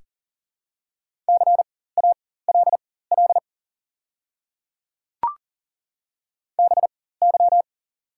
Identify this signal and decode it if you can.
Morse code, heard as a keyed tone, about 40 wpm, spaced as for 19 wpm, 690 Hz. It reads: CALL BY
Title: 4x Character Spacing - Top 100 Words - 40wpm